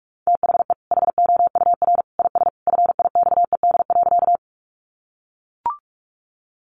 Morse code, heard as a keyed tone, about 44 wpm, 700 Hz. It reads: THE HOUR IS FIXED.
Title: Sentences from Top 1100 Words 45wpm